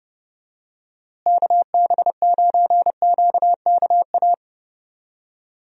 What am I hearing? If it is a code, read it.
Morse code, heard as a keyed tone, about 30 wpm, 700 Hz. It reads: KB9QKA
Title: USA Call Signs - 6 Characters - Repeated 2x - 30wpm